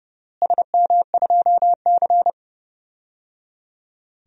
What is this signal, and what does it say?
Morse code, heard as a keyed tone, about 30 wpm, 700 Hz. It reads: SM2C
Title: DXCC-100 Call Signs - 4 Characters - Repeated 3x - 30wpm